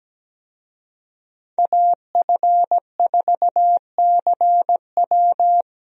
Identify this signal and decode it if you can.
Morse code, heard as a keyed tone, about 17 wpm, 700 Hz. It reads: AF4CW
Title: USA Call Signs - 5 Characters - Repeated 2x - 17wpm